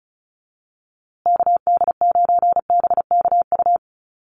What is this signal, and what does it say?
Morse code, heard as a keyed tone, about 34 wpm, 700 Hz. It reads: KD9BKU
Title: USA Call Signs Repeated 2x 35wpm